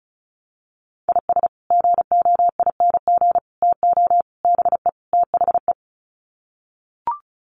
Morse code, heard as a keyed tone, about 34 wpm, 700 Hz. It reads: IS GOING TO BE THE
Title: Sets of 5 Words 35wpm